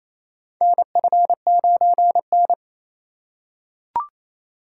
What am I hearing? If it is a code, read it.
Morse code, heard as a keyed tone, about 27 wpm, 700 Hz. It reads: NF9N
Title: USA Call Signs - 4 Characters - 4x Character Spacing - 28wpm